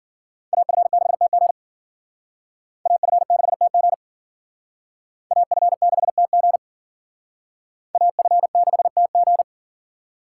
Morse code, heard as a keyed tone, about 51 wpm, 700 Hz. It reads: AF6TG AF6TG AF6TG AF6TG